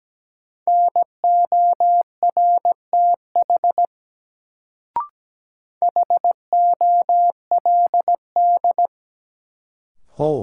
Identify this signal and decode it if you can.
Morse code, heard as a keyed tone, about 17 wpm, 700 Hz. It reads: NORTH HOLD
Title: Sentences from Top 400 Words - New Word Drill - 17wpm